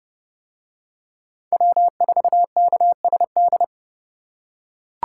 Morse code, heard as a keyed tone, about 30 wpm, 700 Hz. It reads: W4KSD